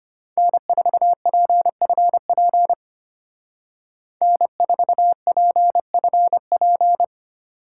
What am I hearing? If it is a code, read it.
Morse code, heard as a keyed tone, about 30 wpm, 700 Hz. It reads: N4PFP N4PFP
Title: Speed Racing - USA Call Signs 25wpm